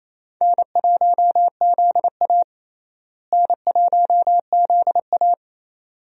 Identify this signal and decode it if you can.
Morse code, heard as a keyed tone, about 28 wpm, 700 Hz. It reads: N1ZA N1ZA
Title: USA Call Signs - 4 Characters - Repeated 2x - 28wpm